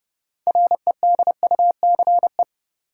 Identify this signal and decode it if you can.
Morse code, heard as a keyed tone, about 31 wpm, 700 Hz. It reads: REDUCE
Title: Sentences from Top 1100 Words - New Word Drill Alt - 30wpm